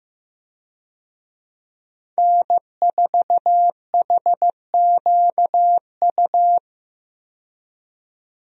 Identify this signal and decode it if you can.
Morse code, heard as a keyed tone, about 15 wpm, 700 Hz. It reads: N4HQU
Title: USA Call Signs - 5 Characters - 15wpm